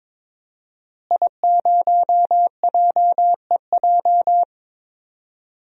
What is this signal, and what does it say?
Morse code, heard as a keyed tone, about 23 wpm, 700 Hz. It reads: I0JEJ
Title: DXCC-100 Call Signs - 5 Characters - Repeated 2x - 22wpm